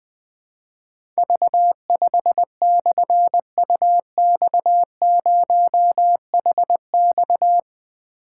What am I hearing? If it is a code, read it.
Morse code, heard as a keyed tone, about 20 wpm, 700 Hz. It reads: V5/UX0HX
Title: Difficult Call Signs 20wpm